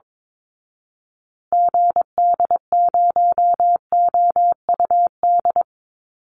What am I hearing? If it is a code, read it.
Morse code, heard as a keyed tone, about 22 wpm, 700 Hz. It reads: GD0OUD
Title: Call Signs Repeated 2x - 22wpm